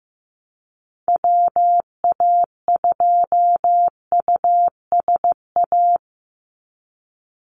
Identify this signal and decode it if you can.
Morse code, heard as a keyed tone, about 15 wpm, 700 Hz. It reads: WA2USA
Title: NA SOTA Activator Callsigns 15wpm